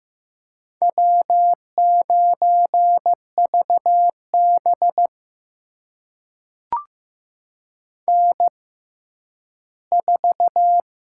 Morse code, heard as a keyed tone, about 15 wpm, spaced as for 10 wpm, 700 Hz. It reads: W9VB N4
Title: USA Call Signs - 4 Characters - 6x Character Spacing - 15wpm